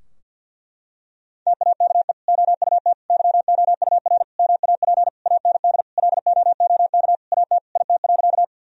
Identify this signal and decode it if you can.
Morse code, heard as a keyed tone, about 46 wpm, 700 Hz. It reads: TAKE OUT YOUR MAP AND LOOK AT IT.
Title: Sentences from Top 1000 Words - Review Part 2 - 50wpm